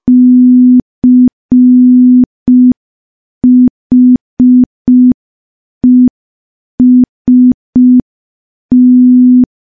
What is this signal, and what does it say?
Morse code, heard as a keyed tone, about 5 wpm, 250 Hz. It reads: CHEST
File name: morse / morse-training-05WPM